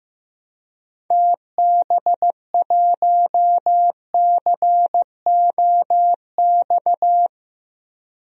Morse code, heard as a keyed tone, about 15 wpm, 700 Hz. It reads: TB1COX
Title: DXCC-100 Call Signs - 6 Characters - 15wpm